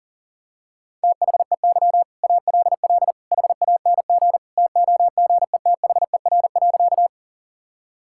Morse code, heard as a keyed tone, about 40 wpm, 700 Hz. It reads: THEY ALL HANG TOGETHER.
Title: Sentences from Top 1600 Words 40wpm